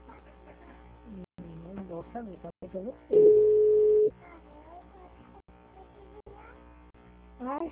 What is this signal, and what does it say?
Ringback tone (Europe, 425 Hz), with speech.